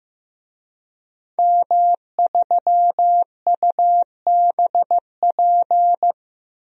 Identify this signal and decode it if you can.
Morse code, heard as a keyed tone, about 15 wpm, 700 Hz. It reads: M3UBP
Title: DXCC-100 Call Signs - 5 Characters - 15wpm